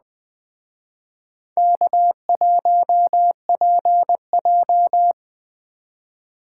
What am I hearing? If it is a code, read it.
Morse code, heard as a keyed tone, about 20 wpm, 700 Hz. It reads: K1PJ